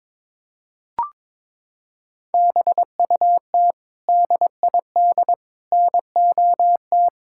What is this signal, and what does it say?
Morse code, heard as a keyed tone, about 22 wpm, 700 Hz. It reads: BUT DID NOT
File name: practice / 22wpm / Sets of 8 Words - 22wpm